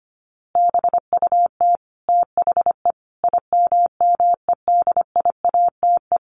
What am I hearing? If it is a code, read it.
Morse code, heard as a keyed tone, about 25 wpm, 700 Hz. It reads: BUT THE IMMEDIATE
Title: A Sound Of Thunder 25wpm